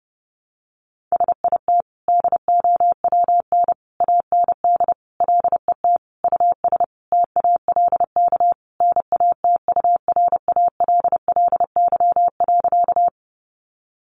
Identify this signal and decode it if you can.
Morse code, heard as a keyed tone, about 30 wpm, 700 Hz. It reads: SIT DOWN AND LET US TALK NATURALLY.
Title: Sentences from Top 1300 Words 30wpm